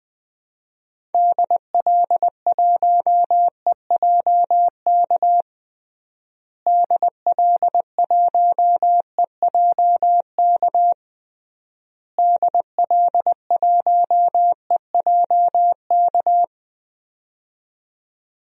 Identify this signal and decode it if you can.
Morse code, heard as a keyed tone, about 20 wpm, 700 Hz. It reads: DL1EJK DL1EJK DL1EJK
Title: DXCC-100 Call Signs - 6 Characters - Repeated 3x - 20wpm